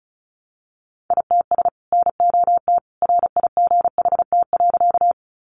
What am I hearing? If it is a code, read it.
Morse code, heard as a keyed tone, about 34 wpm, 700 Hz. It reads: ITS NOT RIGHT.
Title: Sentences from Top 600 Words 35wpm